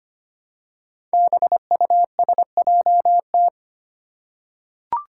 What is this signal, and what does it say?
Morse code, heard as a keyed tone, about 25 wpm, 700 Hz. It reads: BUSJT
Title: Five Challenging Letter-Number Combinations 25wpm